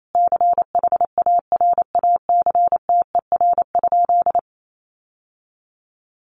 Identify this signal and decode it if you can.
Morse code, heard as a keyed tone, about 28 wpm, 700 Hz. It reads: CHARACTER?
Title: Sentences from Top 1200 Words 28wpm